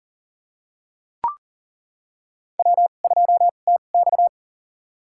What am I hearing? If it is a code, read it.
Morse code, heard as a keyed tone, about 37 wpm, spaced as for 28 wpm, 700 Hz. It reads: W2TX
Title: USA Call Signs - 4 Characters - 2x Character Spacing - 40wpm